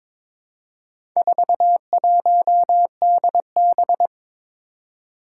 Morse code, heard as a keyed tone, about 22 wpm, 700 Hz. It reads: 41DB
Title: Four Challenging Letter-Number Combinations 22wpm